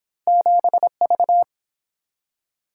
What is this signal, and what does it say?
Morse code, heard as a keyed tone, about 30 wpm, 700 Hz. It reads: 7V